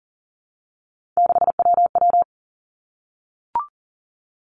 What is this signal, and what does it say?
Morse code, heard as a keyed tone, about 42 wpm, 700 Hz. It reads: 6WW
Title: Three Challenging Letter-Number Combinations 40wpm